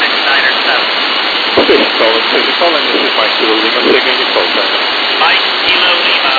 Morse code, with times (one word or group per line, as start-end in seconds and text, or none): none